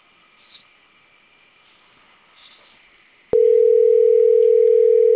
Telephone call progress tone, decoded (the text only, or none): ringback tone